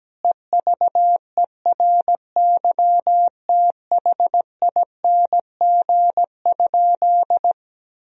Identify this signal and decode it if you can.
Morse code, heard as a keyed tone, about 17 wpm, 700 Hz. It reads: EVERYTHING?
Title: Sentences from Top 1000 Words - Review Part 2 - 17wpm